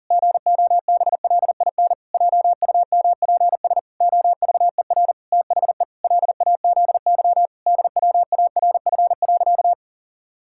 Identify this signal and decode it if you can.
Morse code, heard as a keyed tone, about 40 wpm, 700 Hz. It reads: GOBLIN JUMPS OVER THE LAZY DWARF.
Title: Pangram Challenge v2 - 40wpm